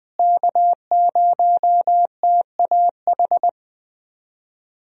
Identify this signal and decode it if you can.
Morse code, heard as a keyed tone, about 21 wpm, 700 Hz. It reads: K0TAH
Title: USA Call Signs - 5 Characters - Repeated 3x - 20wpm